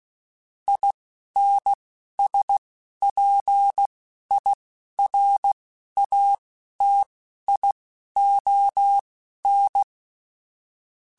Morse code, heard as a keyed tone, about 16 wpm, spaced as for 12 wpm, 800 Hz. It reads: INSPIRATION
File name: Quotes / Quotes-mp3s / Persperation @10